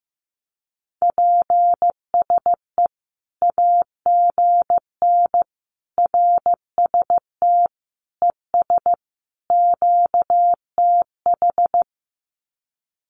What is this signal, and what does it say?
Morse code, heard as a keyed tone, about 15 wpm, 700 Hz. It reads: PSE AGN RST ES QTH
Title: Common QSO Phrases 15wpm